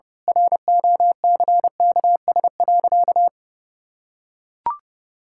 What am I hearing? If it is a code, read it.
Morse code, heard as a keyed tone, about 30 wpm, 700 Hz. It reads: ROCKS.
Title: Pangram Challenge v2 - 30wpm